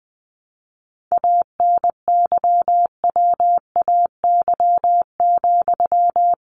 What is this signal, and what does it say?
Morse code, heard as a keyed tone, about 20 wpm, 700 Hz. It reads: ANYWAY,
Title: A Sound Of Thunder 20wpm